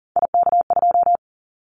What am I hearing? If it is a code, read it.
Morse code, heard as a keyed tone, about 40 wpm, 700 Hz. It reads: IK2
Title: Callsign Prefixes for Top 25 DXCC - 40wpm